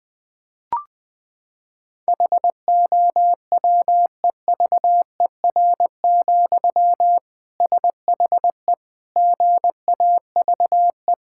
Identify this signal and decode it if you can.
Morse code, heard as a keyed tone, about 20 wpm, 700 Hz. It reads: HOWEVER, SHE GAVE
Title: Sentences from Top 1000 Words - Review Part 6 - 20wpm